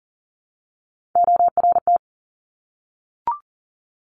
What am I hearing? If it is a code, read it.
Morse code, heard as a keyed tone, about 40 wpm, 690 Hz. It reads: ORT
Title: Top Three Letter Combinations in Words 40wpm